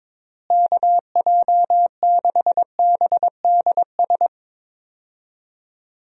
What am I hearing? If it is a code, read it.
Morse code, heard as a keyed tone, about 22 wpm, 700 Hz. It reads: KJ6BDS
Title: USA Call Signs - 6 Characters - Repeated 2x - 22wpm